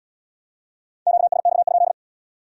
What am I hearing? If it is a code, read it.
Morse code, heard as a keyed tone, about 77 wpm, 700 Hz. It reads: QSX?